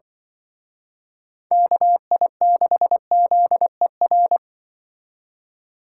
Morse code, heard as a keyed tone, about 24 wpm, 700 Hz. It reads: KI6ZER
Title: Speed Racing - USA 2x3 Call Signs 20wpm